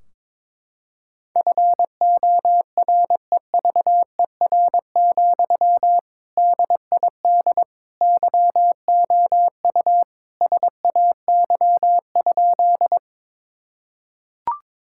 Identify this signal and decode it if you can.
Morse code, heard as a keyed tone, about 22 wpm, 700 Hz. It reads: FOREVER, DID YOU SAY?
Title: Sentences from Top 1600 Words 22wpm